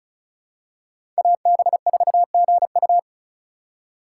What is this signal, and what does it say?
Morse code, heard as a keyed tone, about 36 wpm, 700 Hz. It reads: AB4GU
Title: USA Call Signs - 5 Characters - 4x Character Spacing - 35wpm